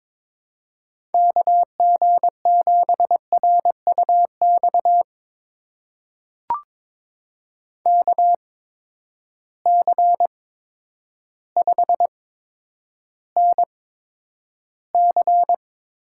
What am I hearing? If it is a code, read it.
Morse code, heard as a keyed tone, about 22 wpm, spaced as for 10 wpm, 700 Hz. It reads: KG7RUX KC5NC
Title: USA Call Signs - 6 Characters - 8x Character Spacing - 22wpm